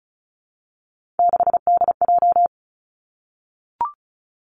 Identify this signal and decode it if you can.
Morse code, heard as a keyed tone, about 36 wpm, 700 Hz. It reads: 6DJ
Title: Three Challenging Letter-Number Combinations 35wpm